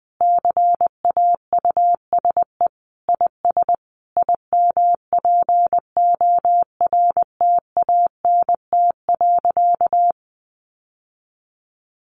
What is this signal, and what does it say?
Morse code, heard as a keyed tone, about 20 wpm, 700 Hz. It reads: CAUSE IS IMPORTANT.
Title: Sentences from Top 1200 Words 20wpm